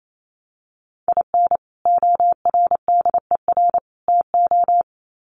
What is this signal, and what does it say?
Morse code, heard as a keyed tone, about 28 wpm, 700 Hz. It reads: IN ORDER TO
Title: Sets of 3 Words 28wpm